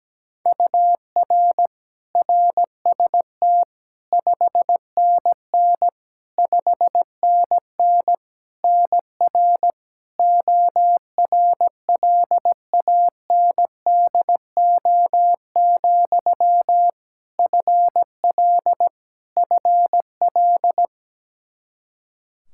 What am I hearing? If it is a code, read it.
Morse code, heard as a keyed tone, about 17 wpm, 700 Hz. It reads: UR RST 5NN 5NN NR ORLANDO, FL FL
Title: Practice QSO - Exchange 1 - Beginning - 17wpm